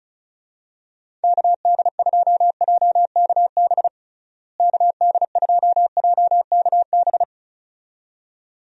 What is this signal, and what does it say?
Morse code, heard as a keyed tone, about 36 wpm, 700 Hz. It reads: KD2JKB KD2JKB
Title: USA Call Signs - 6 Characters - Repeated 2x - 35wpm